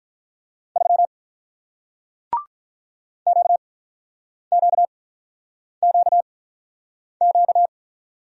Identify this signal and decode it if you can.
Morse code, heard as a keyed tone, about 48 wpm, 700 Hz. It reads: 3 Q Q Q Q